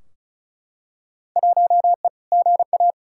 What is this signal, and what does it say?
Morse code, heard as a keyed tone, about 36 wpm, 700 Hz. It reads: 1E GA
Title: Field Day Exchanges - 35wpm